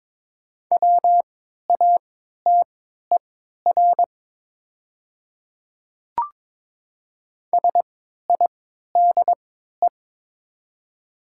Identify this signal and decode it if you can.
Morse code, heard as a keyed tone, about 22 wpm, spaced as for 12 wpm, 700 Hz. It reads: WATER SIDE
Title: 3x Character Spacing - Top 100 Words - 22wpm